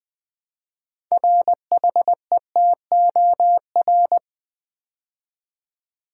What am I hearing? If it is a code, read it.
Morse code, heard as a keyed tone, about 20 wpm, 700 Hz. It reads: RHETOR
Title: Build A Word - 2 Letters At A Time - 8-Letter Words - 20wpm